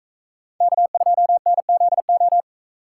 Morse code, heard as a keyed tone, about 44 wpm, 700 Hz. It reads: K2NZO